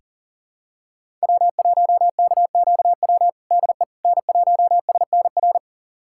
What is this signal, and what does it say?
Morse code, heard as a keyed tone, about 40 wpm, 700 Hz. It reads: W1KQW DE N1SNR
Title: USA Callsign DE Callsign - 5 Characters - 40wpm